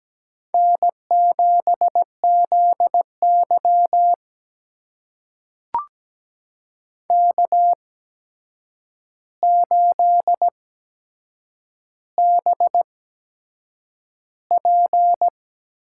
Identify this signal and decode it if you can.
Morse code, heard as a keyed tone, about 17 wpm, spaced as for 7 wpm, 700 Hz. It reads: N7ZY K8BP
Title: USA Call Signs - 8x Character Spacing - 17wpm